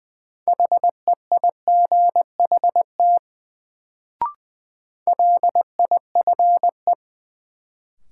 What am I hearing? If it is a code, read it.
Morse code, heard as a keyed tone, about 20 wpm, 700 Hz. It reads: HEIGHT LIFE
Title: Sets of 5 Words - New Word Drill - 20wpm